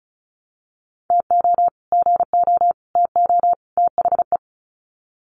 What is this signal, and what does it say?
Morse code, heard as a keyed tone, about 34 wpm, 700 Hz. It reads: TO GO TO THE
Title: Sets of 4 Words 35wpm